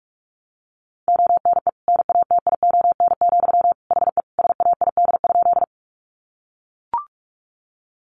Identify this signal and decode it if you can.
Morse code, heard as a keyed tone, about 44 wpm, 700 Hz. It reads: ONE NATION, HE SAID?
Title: Sentences from Top 900 Words 45wpm